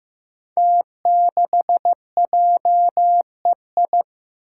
Morse code, heard as a keyed tone, about 15 wpm, 700 Hz. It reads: T6JEI
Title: Five Challenging Letter-Number Combinations 15wpm